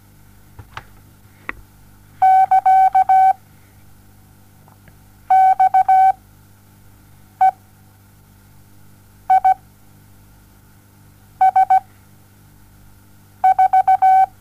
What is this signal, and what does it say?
Morse code, heard as a keyed tone, about 17 wpm, spaced as for 6 wpm, 760 Hz. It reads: <KA> XE I S 4
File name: Onlinelehrgang / morsen / DJ4UF Morsen07